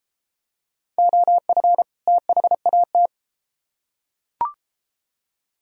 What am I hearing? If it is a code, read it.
Morse code, heard as a keyed tone, about 30 wpm, 700 Hz. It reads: OF THAT